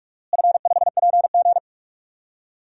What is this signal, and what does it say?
Morse code, heard as a keyed tone, about 41 wpm, 700 Hz. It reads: F5PG
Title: DXCC-100 Call Signs - 4 Characters - Repeated 2x - 45wpm